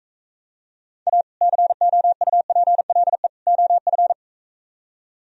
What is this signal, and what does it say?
Morse code, heard as a keyed tone, about 45 wpm, 700 Hz. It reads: A COUPLE OF